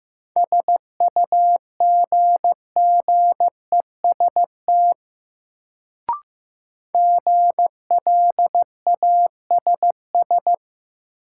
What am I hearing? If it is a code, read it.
Morse code, heard as a keyed tone, about 15 wpm, 700 Hz. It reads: SUGGEST GLASS
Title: Sentences from Top 1300 Words - New Word Drill Alt - 15wpm